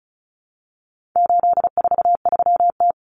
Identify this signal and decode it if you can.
Morse code, heard as a keyed tone, about 34 wpm, 700 Hz. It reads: 843T
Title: Common Cut Numbers - 4-Digit 35wpm